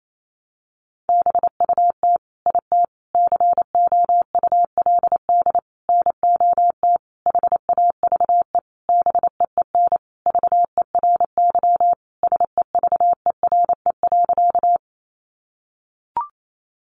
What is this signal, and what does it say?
Morse code, heard as a keyed tone, about 28 wpm, 700 Hz. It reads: BUT IT COULD NOT HAVE BEEN VERY SEVERE.
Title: Sentences from Top 1400 Words 28wpm